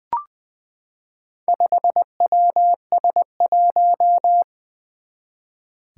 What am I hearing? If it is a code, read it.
Morse code, heard as a keyed tone, about 20 wpm, 700 Hz. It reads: 5WS1
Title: Four Challenging Letter-Number Combinations 20wpm